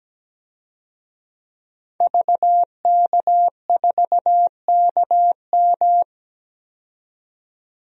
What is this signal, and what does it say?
Morse code, heard as a keyed tone, about 17 wpm, 700 Hz. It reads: VK4KM
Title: DXCC-100 Call Signs - 5 Characters - 17wpm